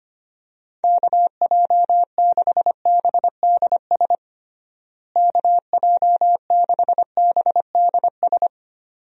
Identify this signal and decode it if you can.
Morse code, heard as a keyed tone, about 25 wpm, 700 Hz. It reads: KJ6BDS KJ6BDS
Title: USA Call Signs - 6 Characters - Repeated 2x - 25wpm